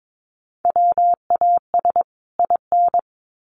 Morse code, heard as a keyed tone, about 22 wpm, 700 Hz. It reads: WAS IN
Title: Sets of 2 Words 22wpm